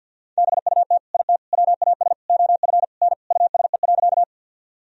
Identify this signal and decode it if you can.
Morse code, heard as a keyed tone, about 51 wpm, 700 Hz. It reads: BUT IT WAS OF N USE.
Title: Sentences from Top 100 Words and Abbreviations 50wpm